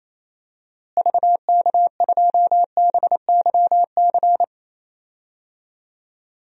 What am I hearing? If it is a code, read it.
Morse code, heard as a keyed tone, about 27 wpm, 700 Hz. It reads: VK2BYC
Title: DXCC-100 Call Signs - 6 Characters - Repeated 3x - 28wpm